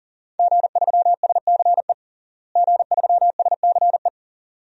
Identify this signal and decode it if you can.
Morse code, heard as a keyed tone, about 40 wpm, 700 Hz. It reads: G3SCE G3SCE
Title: DXCC-100 Call Signs - 5 Characters - Repeated 2x - 40wpm